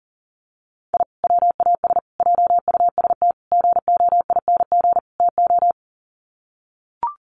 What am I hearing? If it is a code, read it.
Morse code, heard as a keyed tone, about 37 wpm, 700 Hz. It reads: I WAS JUST GOING TO
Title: Sets of 5 Words Encore 40wpm